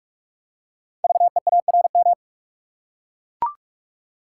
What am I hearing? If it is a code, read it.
Morse code, heard as a keyed tone, about 44 wpm, 690 Hz. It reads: VEARM
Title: Code Groups - 5 Letters - 45wpm